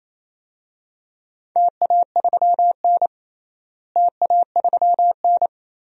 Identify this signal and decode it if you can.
Morse code, heard as a keyed tone, about 28 wpm, 700 Hz. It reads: TA3N TA3N
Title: DXCC-100 Call Signs - 4 Characters - Repeated 2x - 28wpm